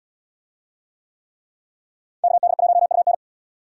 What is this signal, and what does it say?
Morse code, heard as a keyed tone, about 77 wpm, 700 Hz. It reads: KD1UN